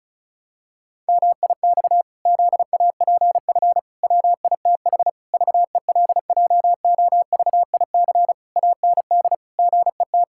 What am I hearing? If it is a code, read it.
Morse code, heard as a keyed tone, about 36 wpm, 700 Hz. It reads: MIX ZAPF WITH VELJOVIC AND GET
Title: Pangram Challenge v2 - 35wpm